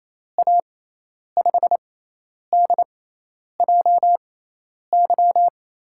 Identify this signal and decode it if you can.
Morse code, heard as a keyed tone, about 28 wpm, spaced as for 10 wpm, 700 Hz. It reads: A5DJY
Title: USA Call Signs - 6x Character Spacing - 28wpm